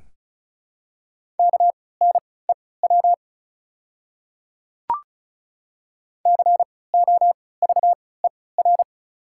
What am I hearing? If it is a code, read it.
Morse code, heard as a keyed tone, about 34 wpm, spaced as for 20 wpm, 700 Hz. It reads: KNEW COVER
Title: 3x Character Spacing - Top 500 Words - 35wpm